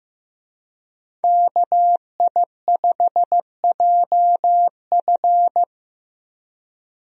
Morse code, heard as a keyed tone, about 15 wpm, 700 Hz. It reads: KI5JF
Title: USA Call Signs - 5 Characters - Repeated 3x - 15wpm